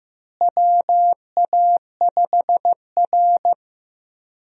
Morse code, heard as a keyed tone, about 15 wpm, 700 Hz. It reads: WA5R
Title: USA Call Signs - 4 Characters - Repeated 2x - 15wpm